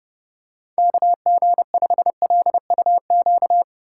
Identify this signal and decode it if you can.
Morse code, heard as a keyed tone, about 29 wpm, 700 Hz. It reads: KG5LUQ